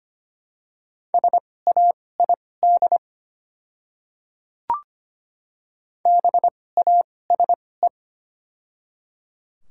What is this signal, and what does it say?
Morse code, heard as a keyed tone, about 25 wpm, spaced as for 18 wpm, 700 Hz. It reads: SAID BASE
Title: 2x Character Spacing - Top 300 Words - 25wpm